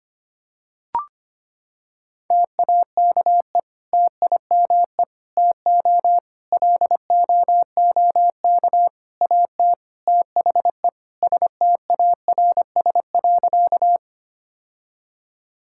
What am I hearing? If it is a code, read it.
Morse code, heard as a keyed tone, about 25 wpm, 700 Hz. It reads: TAKE TIME TO LOOK AT THE STARS.
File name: practice / 25wpm / Sentences with 3-Word Phrases - 25wpm